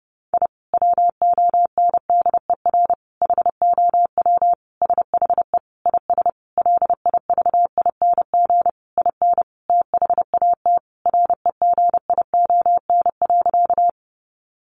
Morse code, heard as a keyed tone, about 30 wpm, 700 Hz. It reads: I WONDER HOW SHE IS LIVING IN THAT REGION.
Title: Sentences from Top 1500 Words 30wpm